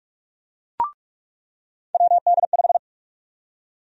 Morse code, heard as a keyed tone, about 45 wpm, 700 Hz. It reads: WD5
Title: USA Call Sign Prefixes - 45wpm